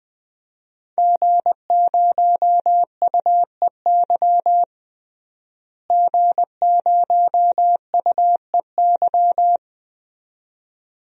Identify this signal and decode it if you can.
Morse code, heard as a keyed tone, about 21 wpm, 700 Hz. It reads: G0UEY G0UEY
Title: DXCC-100 Call Signs - 5 Characters - Repeated 2x - 20wpm